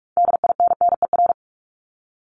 Morse code, heard as a keyed tone, about 46 wpm, 700 Hz. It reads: DINNER
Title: Multi-letter Phonemes 45wpm